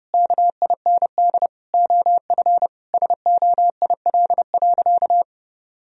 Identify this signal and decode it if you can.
Morse code, heard as a keyed tone, about 30 wpm, 700 Hz. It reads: KIND OF SOIL.
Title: Sentences from Top 1000 Words - Review Part 4 - 30wpm